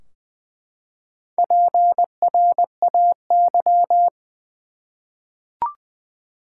Morse code, heard as a keyed tone, about 20 wpm, 700 Hz. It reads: PRAY
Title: Four Letter Words - 20wpm